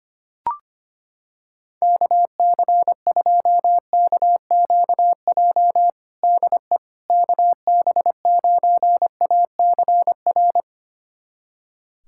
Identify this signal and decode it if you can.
Morse code, heard as a keyed tone, about 25 wpm, 700 Hz. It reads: KC2KQJ DE KB9ACR
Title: USA Callsign DE Callsign - 6 Characters - 25wpm